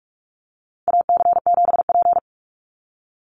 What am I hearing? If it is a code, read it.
Morse code, heard as a keyed tone, about 44 wpm, 700 Hz. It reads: AC7P
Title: NA SOTA Chaser Callsigns 45wpm